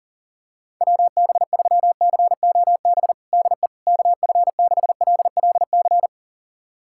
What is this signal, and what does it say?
Morse code, heard as a keyed tone, about 37 wpm, 700 Hz. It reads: WB3COB DE KF6LLC
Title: USA Callsign DE Callsign - 6 Characters - 40wpm